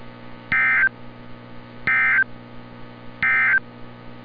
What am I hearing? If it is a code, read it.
Emergency Alert System (SAME) tones, heard as harsh buzzing bursts. It is heard from 0.4 to 3.7 s.
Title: NOAA Weather Radio - Altoona, PA